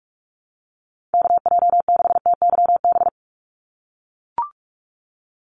Morse code, heard as a keyed tone, about 44 wpm, 700 Hz. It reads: KJ6TYB
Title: USA Call Signs 45wpm